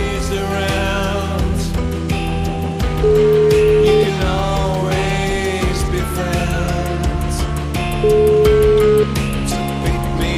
Ringback tone (Europe, 425 Hz), with noise.